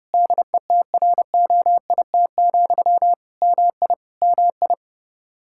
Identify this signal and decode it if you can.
Morse code, heard as a keyed tone, about 32 wpm, 700 Hz. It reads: DETROIT, MI MI
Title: Practice QSO - Exchange 1 - Beginning - 30wpm